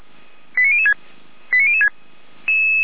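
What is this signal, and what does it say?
Five-tone selective call (ZVEI), heard as quick runs of five tones, tones of about 70 ms. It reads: 89006 89006
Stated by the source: ZVEI five-tone call 89285 received on an SDR.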